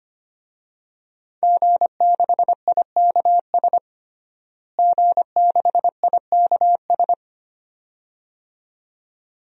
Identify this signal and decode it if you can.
Morse code, heard as a keyed tone, about 25 wpm, 700 Hz. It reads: G6IKS G6IKS
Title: DXCC-100 Call Signs - 5 Characters - Repeated 2x - 25wpm